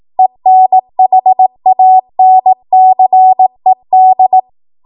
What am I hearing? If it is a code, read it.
Morse code, heard as a keyed tone, about 18 wpm, 750 Hz. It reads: ENHANCED
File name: Morse / 18WPM